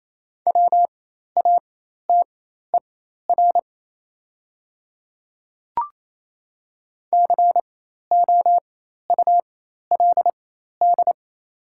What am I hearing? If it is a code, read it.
Morse code, heard as a keyed tone, about 28 wpm, spaced as for 13 wpm, 700 Hz. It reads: WATER COULD
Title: 4x Character Spacing - Top 100 Words - 28wpm